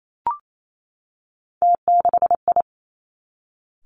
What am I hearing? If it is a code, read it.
Morse code, heard as a keyed tone, about 28 wpm, 700 Hz. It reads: T6I
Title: Three Challenging Letter-Number Combinations 28wpm